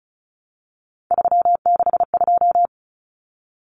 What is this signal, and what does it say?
Morse code, heard as a keyed tone, about 34 wpm, 700 Hz. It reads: 362
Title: Common Cut Numbers - 3-Digit 35wpm